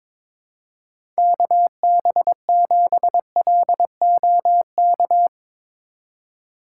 Morse code, heard as a keyed tone, about 22 wpm, 700 Hz. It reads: KB7LOK
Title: USA Call Signs - 2x Character Spacing - 22wpm